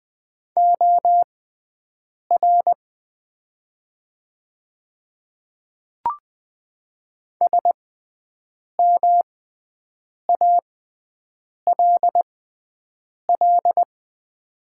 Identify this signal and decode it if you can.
Morse code, heard as a keyed tone, about 20 wpm, spaced as for 7 wpm, 700 Hz. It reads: OR SMALL
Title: 6x Character Spacing - Top 300 Words - 20wpm